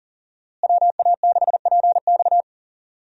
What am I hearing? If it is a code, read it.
Morse code, heard as a keyed tone, about 40 wpm, 700 Hz. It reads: WA6PX
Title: USA Call Signs - 5 Characters - Repeated 3x - 40wpm